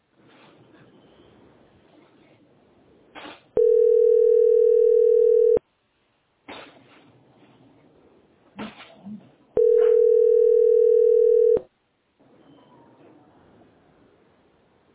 Ringback tone (North America), 440 + 480 Hz.
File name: ad9a1e45-680d-4fc3-87ee- f621c